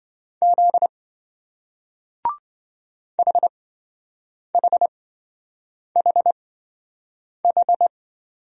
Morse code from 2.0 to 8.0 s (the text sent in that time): HHHH